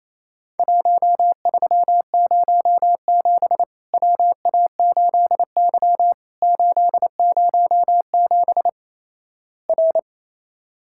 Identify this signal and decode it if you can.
Morse code, heard as a keyed tone, about 28 wpm, 700 Hz. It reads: 1307 WA8Y 807 R
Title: Worked All Europe DX Contest - QTC Exchange - 28wpm